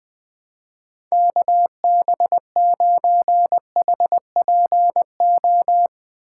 Morse code, heard as a keyed tone, about 20 wpm, 700 Hz. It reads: KB9HPO
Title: USA Call Signs - 6 Characters - 3x Character Spacing - 20wpm